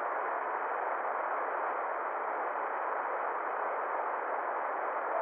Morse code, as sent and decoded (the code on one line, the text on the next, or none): none